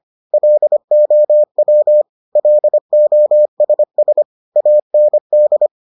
Morse code, heard as a keyed tone, about 25 wpm, 600 Hz. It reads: LOW LOSS AND